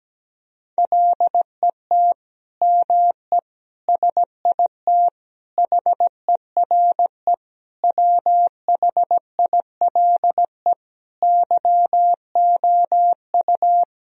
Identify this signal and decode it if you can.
Morse code, heard as a keyed tone, about 17 wpm, 700 Hz. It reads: LET ME SIT HERE WHILE YOU
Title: Sentences from Top 1000 Words - Review Part 1 - 17wpm